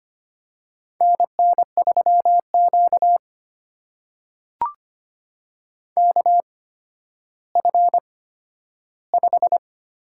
Morse code, heard as a keyed tone, about 25 wpm, spaced as for 12 wpm, 700 Hz. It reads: NN3Q KF5
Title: USA Call Signs - 8x Character Spacing - 25wpm